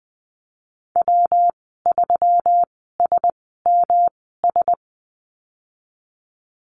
Morse code, heard as a keyed tone, about 20 wpm, spaced as for 15 wpm, 700 Hz. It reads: W3SMS
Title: 161 Lesson 26g - Letter g - Call Signs 20wpm